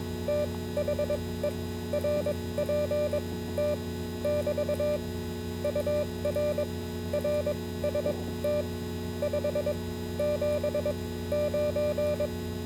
Morse code, heard as a keyed tone, about 22 wpm, spaced as for 16 wpm, 590 Hz, with noise. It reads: THE RPT <BT> UR RST 579